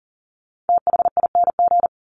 Morse code, heard as a keyed tone, about 40 wpm, 700 Hz. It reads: THING
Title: Top 300 Words 40wpm